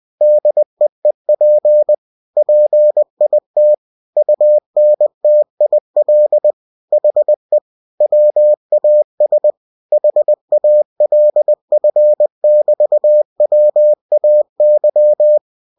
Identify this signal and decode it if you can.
Morse code, heard as a keyed tone, about 20 wpm, 600 Hz. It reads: DEEP PIT UNTIL HE WAS HALF<BT>WAY
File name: winnie-20wpm